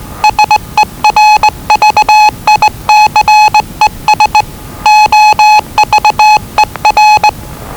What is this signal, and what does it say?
Morse code, heard as a keyed tone, about 20 wpm, 850 Hz. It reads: SERVICES OVER
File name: WellingtonRadioZlwCloseDownOnHfAndMf / 01-Zlw 4277khz 417.5 qrt until 0925z